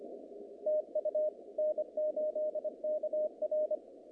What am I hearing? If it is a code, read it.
Morse code, heard as a keyed tone, about 24 wpm, 610 Hz. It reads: TU N8KR
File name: contests / naqp-cw-k5tr-op-k5pi- -2300